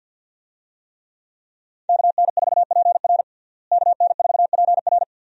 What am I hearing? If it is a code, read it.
Morse code, heard as a keyed tone, about 51 wpm, 700 Hz. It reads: KN4PR KN4PR
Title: USA Call Signs - 5 Characters - Repeated 2x - 50wpm